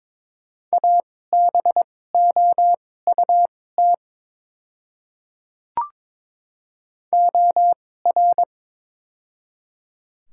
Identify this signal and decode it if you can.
Morse code, heard as a keyed tone, about 22 wpm, spaced as for 16 wpm, 700 Hz. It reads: ABOUT OR
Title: 2x Character Spacing - Top 300 Words - 22wpm